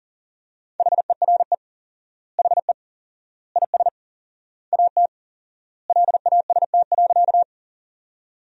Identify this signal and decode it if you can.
Morse code, heard as a keyed tone, about 40 wpm, 700 Hz. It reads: HERE HE IS AT LAST.
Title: 4x Word-Spacing - Sentences from Top 300 Words - 40wpm